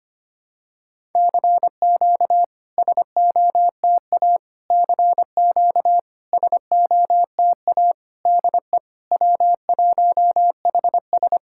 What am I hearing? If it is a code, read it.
Morse code, heard as a keyed tone, about 25 wpm, 700 Hz. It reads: CQ SOTA CQ SOTA DE W1HS
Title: SOTA QSO Practice 25wpm